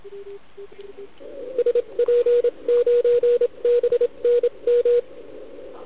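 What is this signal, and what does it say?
Morse code, heard as a keyed tone, about 27 wpm, spaced as for 24 wpm, 480 Hz, with radio noise. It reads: SP9BNM
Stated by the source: CW from the OKOM DX contest on 80 m, received with a TS480.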